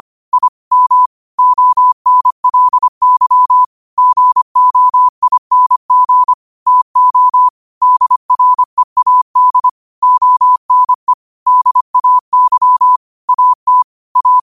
Morse code, heard as a keyed tone, about 25 wpm, 1000 Hz. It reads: I M ONLY GOING TO DREAD ONE DAY AT A
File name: qotdfiles / 25wpm